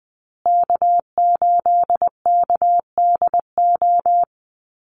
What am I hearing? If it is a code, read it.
Morse code, heard as a keyed tone, about 20 wpm, 700 Hz. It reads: K8KDO
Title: USA Call Signs Repeated 2x 20wpm